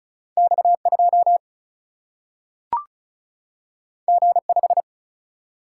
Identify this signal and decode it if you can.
Morse code, heard as a keyed tone, about 36 wpm, 700 Hz. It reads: X 2 G 5
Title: Single-Letter Single-Number 35wpm